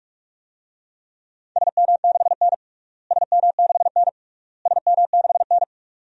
Morse code, heard as a keyed tone, about 46 wpm, 700 Hz. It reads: SM6N SM6N SM6N
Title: DXCC-100 Call Signs - 4 Characters - Repeated 3x - 45wpm